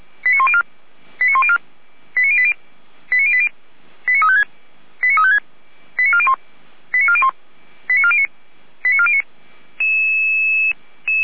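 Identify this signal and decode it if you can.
Five-tone selective call (ZVEI), heard as quick runs of five tones, tones of about 70 ms. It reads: 89104 89104 89080 89080 89356 89356 89401 89401 89409 89409
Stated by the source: ZVEI five-tone selective call 89350, received with an SDR.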